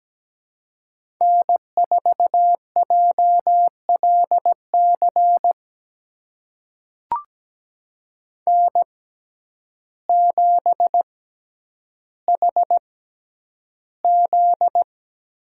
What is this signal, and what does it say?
Morse code, heard as a keyed tone, about 17 wpm, spaced as for 9 wpm, 700 Hz. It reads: N4JLC N7HZ
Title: USA Call Signs - 5 Characters - 6x Character Spacing - 17wpm